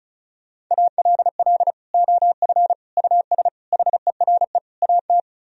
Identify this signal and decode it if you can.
Morse code, heard as a keyed tone, about 44 wpm, 700 Hz. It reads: ALL OF US HERE AT